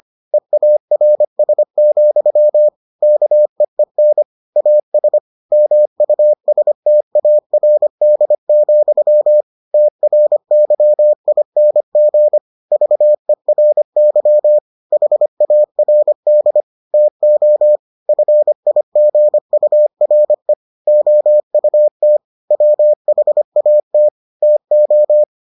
Morse code, heard as a keyed tone, about 25 wpm, 600 Hz. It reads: EARS, KEEN AS MUSTARD, TRYING VERY HARD TO FIGURE OUT WHAT TO